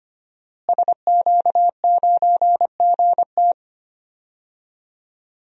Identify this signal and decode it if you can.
Morse code, heard as a keyed tone, about 25 wpm, 700 Hz. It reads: SQ9GT
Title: DXCC-100 Call Signs - 5 Characters - Repeated 3x - 25wpm